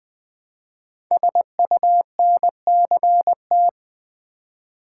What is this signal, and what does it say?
Morse code, heard as a keyed tone, about 20 wpm, 700 Hz. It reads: SUNCT
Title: Code Groups - 5 Letter:Numbers - 20wpm